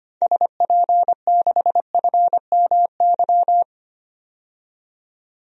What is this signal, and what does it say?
Morse code, heard as a keyed tone, about 25 wpm, 700 Hz. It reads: SP6FMY
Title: DXCC-100 Call Signs - 6 Characters - 25wpm